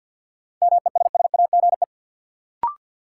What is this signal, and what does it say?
Morse code, heard as a keyed tone, about 52 wpm, 700 Hz. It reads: MESSAGE
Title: Sentences from Top 1100 Words - New Word Drill Alt - 50wpm